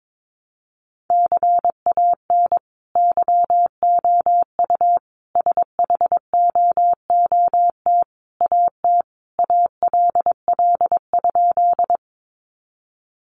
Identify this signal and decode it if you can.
Morse code, heard as a keyed tone, about 22 wpm, 700 Hz. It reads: CAN YOU SHOOT AT ALL?
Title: Sentences from Top 1200 Words 22wpm